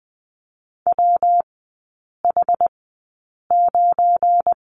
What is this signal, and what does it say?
Morse code, heard as a keyed tone, about 20 wpm, spaced as for 8 wpm, 700 Hz. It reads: WH9
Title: 172 Lesson 28e - Number 9 - Three Characters 20wpm